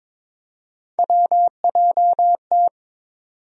Morse code, heard as a keyed tone, about 22 wpm, 700 Hz. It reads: WJT
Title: USA Call Sign Postfixes - 3 Letters 22wpm